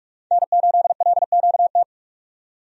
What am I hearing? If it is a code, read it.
Morse code, heard as a keyed tone, about 46 wpm, 700 Hz. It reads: N8LQT
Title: USA Call Signs - 5 Characters - 45wpm